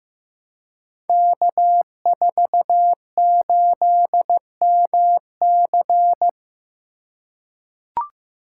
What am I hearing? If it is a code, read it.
Morse code, heard as a keyed tone, about 15 wpm, 700 Hz. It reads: K48MC
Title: Code Groups - 5 Letter:Numbers - 15wpm